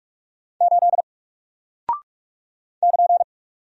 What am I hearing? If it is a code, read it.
Morse code, heard as a keyed tone, about 44 wpm, 700 Hz. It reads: 8 <KN>
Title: Single Character (Letter:Number:Punctuation:Prosign) Alt 45wpm